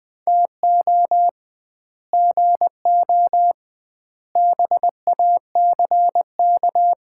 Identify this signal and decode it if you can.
Morse code, heard as a keyed tone, about 20 wpm, 700 Hz. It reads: TO GO BACK
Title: 2x Word-Spacing - Sets of 5 Words - 20wpm